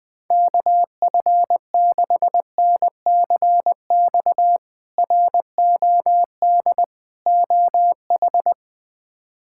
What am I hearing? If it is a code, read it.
Morse code, heard as a keyed tone, about 20 wpm, 700 Hz. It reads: KF6NCX ROD OH
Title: K1USN SST QSO Practice - 20wpm